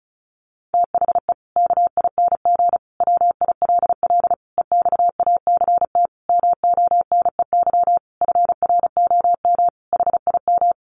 Morse code, heard as a keyed tone, about 34 wpm, 700 Hz. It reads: THE KING WILL EXACT MONEY FROM HIM
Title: Sentences from Top 1300 Words 35wpm